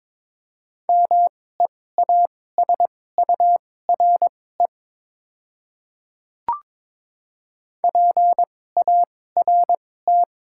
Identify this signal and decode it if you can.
Morse code, heard as a keyed tone, about 23 wpm, spaced as for 16 wpm, 700 Hz. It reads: MEASURE PART
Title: 2x Character Spacing - Top 500 Words - 22wpm